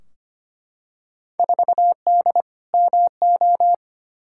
Morse code, heard as a keyed tone, about 25 wpm, 700 Hz. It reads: 4D MO
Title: Field Day Exchanges Encore - 25wpm